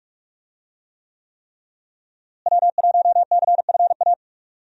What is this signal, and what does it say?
Morse code, heard as a keyed tone, about 47 wpm, 700 Hz. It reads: W1CFA
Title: USA Call Signs - 5 Characters - Repeated 3x - 45wpm